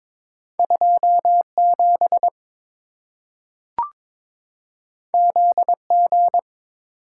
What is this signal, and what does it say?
Morse code, heard as a keyed tone, about 22 wpm, 700 Hz. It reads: 27 ZG
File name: practice / 22wpm / Two Letter-Number Combinations - 22wpm